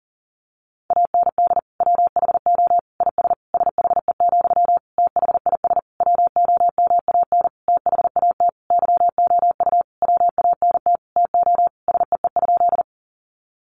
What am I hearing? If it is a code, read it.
Morse code, heard as a keyed tone, about 40 wpm, 700 Hz. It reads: AND WHO IS SHE, THIS WOMAN THAT YOU WANT TO SEE?
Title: Sentences from Top 1100 Words 40wpm